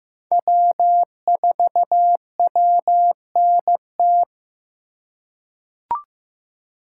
Morse code, heard as a keyed tone, about 15 wpm, 700 Hz. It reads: W4WNT
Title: K1USN SST QSO Practice - 15wpm